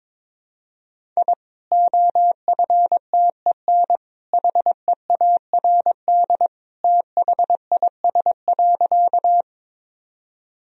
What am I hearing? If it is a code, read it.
Morse code, heard as a keyed tone, about 22 wpm, 700 Hz. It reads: I OFTEN HEARD THIS.
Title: Sentences from Top 1000 Words - Review Part 1 - 22wpm